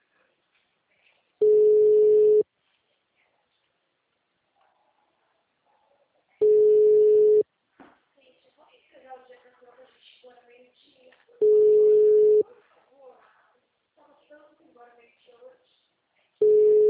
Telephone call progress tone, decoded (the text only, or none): ringback tone